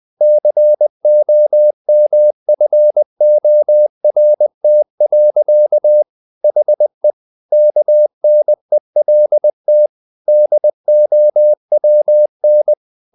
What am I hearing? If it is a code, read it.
Morse code, heard as a keyed tone, about 20 wpm, 600 Hz. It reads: COMFORT. HE KNELT DOWN
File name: the old man and the sea / the old man and the sea part 1 20wpm